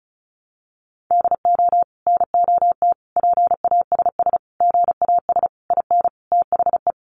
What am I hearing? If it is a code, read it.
Morse code, heard as a keyed tone, about 34 wpm, 700 Hz. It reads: DO NOT PASS GAS IN THE
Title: Sentences from Top 1100 Words 35wpm